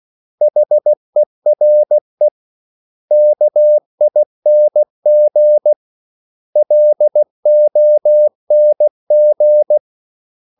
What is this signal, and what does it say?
Morse code, heard as a keyed tone, about 16 wpm, 600 Hz. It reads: HERE KING LONG
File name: CW / Practice / 500MostCommonEnglishWords16